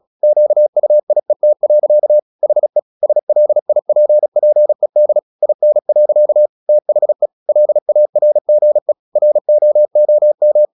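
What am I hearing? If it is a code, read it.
Morse code, heard as a keyed tone, about 36 wpm, 600 Hz. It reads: QUIET. HE SLIPPED IN. THE LARGE ROOM
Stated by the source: Morse practice at 36 wpm, text of Ken Follett's novel Never, from CW med Gnister.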